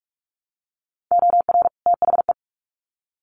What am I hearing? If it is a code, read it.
Morse code, heard as a keyed tone, about 44 wpm, 700 Hz. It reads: OR THE
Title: Sets of 2 Words 45wpm